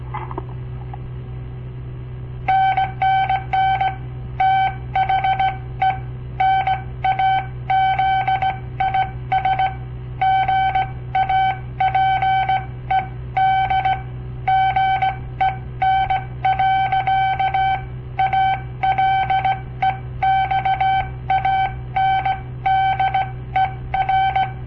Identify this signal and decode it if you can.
Morse code, heard as a keyed tone, about 18 wpm, spaced as for 15 wpm, 770 Hz. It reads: NNN THE NAZIS GAPED GEN. ALEXANDER
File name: papertapes / local / Code Practice-01